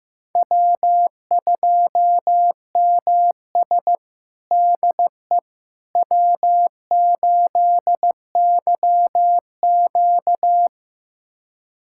Morse code, heard as a keyed tone, about 15 wpm, 700 Hz. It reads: W2MS DE W8YQ
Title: USA Callsign DE Callsign - 4 Characters - 15wpm